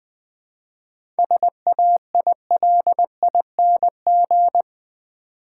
Morse code, heard as a keyed tone, about 20 wpm, 700 Hz. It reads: SAILING
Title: 7 Letter Words - First 3 Letters Given - 20wpm